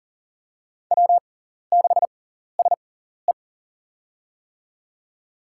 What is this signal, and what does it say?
Morse code, heard as a keyed tone, about 37 wpm, spaced as for 14 wpm, 700 Hz. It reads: W6SE
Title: USA Call Signs - 4 Characters - 6x Character Spacing - 40wpm